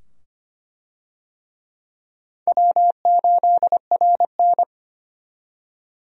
Morse code, heard as a keyed tone, about 25 wpm, 700 Hz. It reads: W8RN
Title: USA Call Signs - 4 Characters - 25wpm